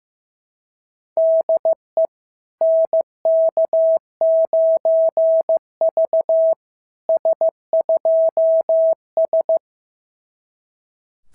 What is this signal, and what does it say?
Morse code, heard as a keyed tone, about 14 wpm, 650 Hz. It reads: DE NK9V S2S
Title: SOTA QSO Practice 15wpm